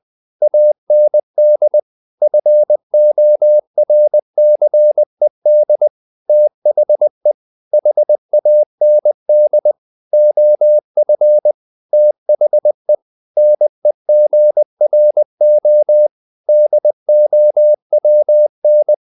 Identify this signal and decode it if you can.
Morse code, heard as a keyed tone, about 20 wpm, 600 Hz. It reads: AND FORCED THE HAND OF THE NEGRO DOWN
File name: the old man and the sea part 2 20wpm